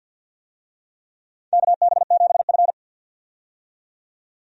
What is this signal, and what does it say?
Morse code, heard as a keyed tone, about 53 wpm, 700 Hz. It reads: KB7F
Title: USA Call Signs - 4 Characters - 50wpm